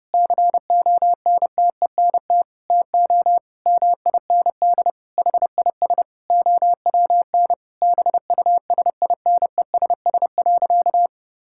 Morse code, heard as a keyed tone, about 30 wpm, 700 Hz. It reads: CONTENT TO MIND HIS OWN BUSINESS.
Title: Sentences from Top 1000 Words - Review Part 6 - 30wpm